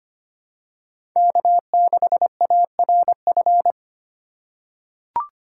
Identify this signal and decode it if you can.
Morse code, heard as a keyed tone, about 25 wpm, 700 Hz. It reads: K6ARF
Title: USA Call Signs - 5 Characters - 4x Character Spacing - 25wpm